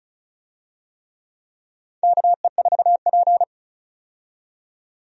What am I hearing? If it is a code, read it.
Morse code, heard as a keyed tone, about 36 wpm, 700 Hz. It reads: KE4P
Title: USA Call Signs - 4 Characters - 35wpm